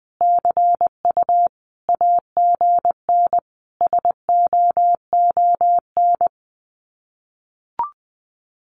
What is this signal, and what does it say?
Morse code, heard as a keyed tone, about 20 wpm, 700 Hz. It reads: CU AGN SOON
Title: Common QSO Phrases 20wpm